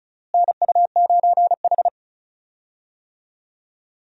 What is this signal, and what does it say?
Morse code, heard as a keyed tone, about 34 wpm, 700 Hz. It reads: NU9H
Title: USA Call Signs - 4 Characters - Repeated 3x - 35wpm